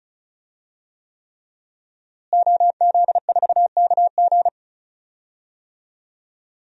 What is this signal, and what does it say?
Morse code, heard as a keyed tone, about 36 wpm, 700 Hz. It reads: OZ4KG
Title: DXCC-100 Call Signs - 5 Characters - Repeated 3x - 35wpm